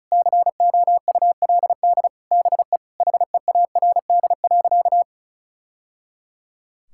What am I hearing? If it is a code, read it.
Morse code, heard as a keyed tone, about 34 wpm, 700 Hz. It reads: COULD BE HEARD.
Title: Sentences from Top 1000 Words - Review Part 5 - 35wpm